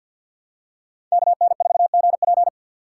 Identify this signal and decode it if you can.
Morse code, heard as a keyed tone, about 51 wpm, 700 Hz. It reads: KN4GP
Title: USA Call Signs - 5 Characters - 4x Character Spacing - 50wpm